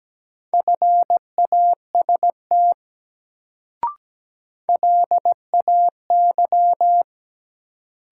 Morse code, heard as a keyed tone, about 17 wpm, 700 Hz. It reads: FAST LAY
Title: Sentences from Top 400 Words - New Word Drill - 17wpm